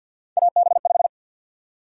Morse code, heard as a keyed tone, about 53 wpm, 700 Hz. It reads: AB5
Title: USA Call Sign Prefixes - 50wpm